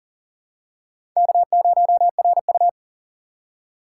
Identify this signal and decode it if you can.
Morse code, heard as a keyed tone, about 40 wpm, 700 Hz. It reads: K0RU
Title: USA Call Signs - 4x Character Spacing - 40wpm